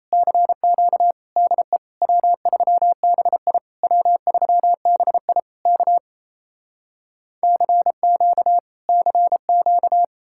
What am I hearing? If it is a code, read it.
Morse code, heard as a keyed tone, about 37 wpm, 700 Hz. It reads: CQ DE W3BI W3BI K CQ CQ